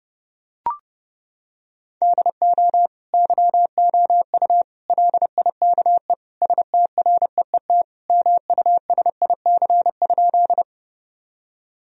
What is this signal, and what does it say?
Morse code, heard as a keyed tone, about 32 wpm, 700 Hz. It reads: DO YOU LIKE STREET MUSIC?
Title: Sentences from Top 1000 Words - Review Part 3 - 30wpm